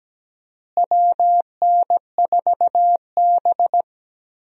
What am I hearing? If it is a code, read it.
Morse code, heard as a keyed tone, about 17 wpm, 700 Hz. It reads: WN4B
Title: USA Call Signs - 4 Characters - Repeated 3x - 17wpm